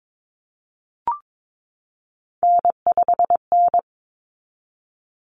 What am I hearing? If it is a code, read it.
Morse code, heard as a keyed tone, about 22 wpm, 700 Hz. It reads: N5N
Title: Common Cut Numbers - 3-Digit 22wpm